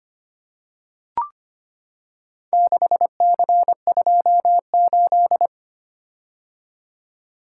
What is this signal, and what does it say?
Morse code, heard as a keyed tone, about 25 wpm, 700 Hz. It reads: 6C28
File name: practice / 25wpm / Four Letter-Number Combinations - 25wpm